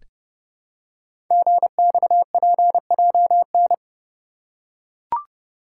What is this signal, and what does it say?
Morse code, heard as a keyed tone, about 30 wpm, 700 Hz. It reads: GXPJN
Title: Code Groups - 5 Letter:Numbers - 30wpm